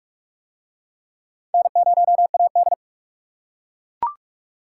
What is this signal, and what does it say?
Morse code, heard as a keyed tone, about 37 wpm, 700 Hz. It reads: N0AD
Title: Speed Racing - USA 1x2 Call Signs 30wpm